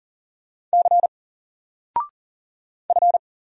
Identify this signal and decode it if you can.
Morse code, heard as a keyed tone, about 40 wpm, 700 Hz. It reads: C F
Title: Single Letter-Number 40wpm